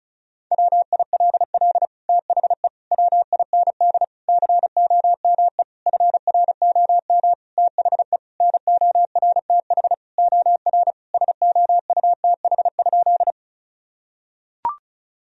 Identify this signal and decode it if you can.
Morse code, heard as a keyed tone, about 34 wpm, 700 Hz. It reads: WILL THE WIND COME FROM THE NORTH OR SOUTH?
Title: Sentences from Top 1000 Words - Review Part 2 - 35wpm